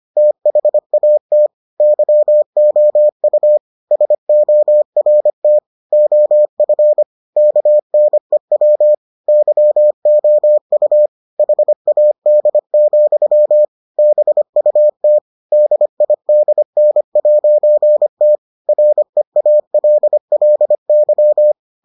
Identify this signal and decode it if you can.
Morse code, heard as a keyed tone, about 25 wpm, 600 Hz. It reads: THAT YOU SORT OF KNEW YOU HAD, BUT DIDN'T REALLY